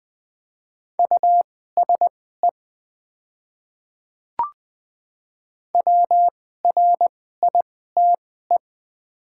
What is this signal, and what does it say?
Morse code, heard as a keyed tone, about 20 wpm, spaced as for 14 wpm, 700 Hz. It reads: USE WRITE
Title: 2x Character Spacing - Top 100 Words - 20wpm